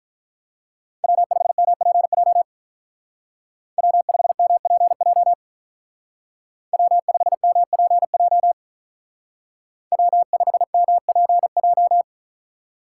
Morse code, heard as a keyed tone, about 43 wpm, 700 Hz. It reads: W5MPJ W5MPJ W5MPJ W5MPJ
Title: Speed Racing - USA 1x3 Call Signs 35wpm